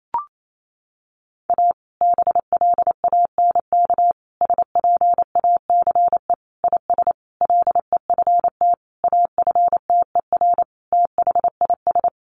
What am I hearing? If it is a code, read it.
Morse code, heard as a keyed tone, about 28 wpm, 700 Hz. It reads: A BLANK SPACE IS LEFT AFTER THIS
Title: Sentences from Top 1500 Words 28wpm